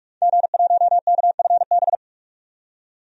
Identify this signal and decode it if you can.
Morse code, heard as a keyed tone, about 43 wpm, 700 Hz. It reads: G1KFB
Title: DXCC-100 Call Signs - 5 Characters - 45wpm